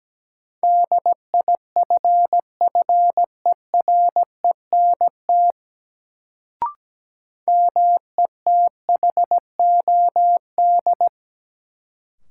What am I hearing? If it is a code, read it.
Morse code, heard as a keyed tone, about 17 wpm, 700 Hz. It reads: DIFFERENT METHOD
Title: Sentences from Top 600 Words - New Word Drill - 17wpm